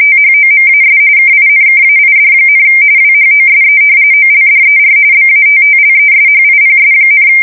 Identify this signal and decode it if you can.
RTTY, about 45.45 baud. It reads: THOMAS KINKAID, WHOSE OLDER BATTLESHIPS,
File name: fsk170 / battleshipnj